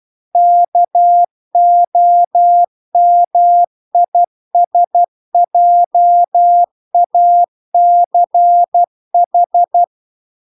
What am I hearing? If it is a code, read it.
Morse code, heard as a keyed tone, about 12 wpm, 700 Hz. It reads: KOMISJACH
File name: 12wpm